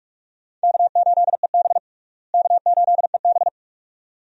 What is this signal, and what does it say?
Morse code, heard as a keyed tone, about 46 wpm, 700 Hz. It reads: K8EB K8EB
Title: USA Call Signs - 4 Characters - Repeated 2x - 45wpm